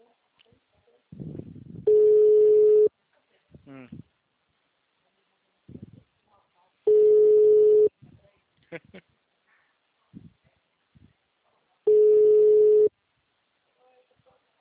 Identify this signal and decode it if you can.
Ringback tone (Europe, 425 Hz), with speech.